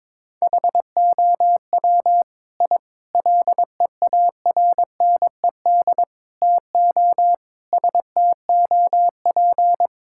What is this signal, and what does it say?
Morse code, heard as a keyed tone, about 22 wpm, 700 Hz. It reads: HOW I LEARNED TO STOP
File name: practice / 22wpm / Sets of 8 Words - 22wpm